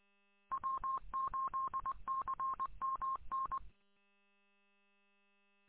Morse code, heard as a keyed tone, about 20 wpm, 1070 Hz. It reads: W8CMN